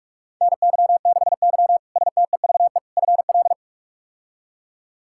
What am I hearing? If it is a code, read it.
Morse code, heard as a keyed tone, about 46 wpm, 700 Hz. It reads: NY6Y STEVE FL
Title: K1USN SST QSO Practice - 45wpm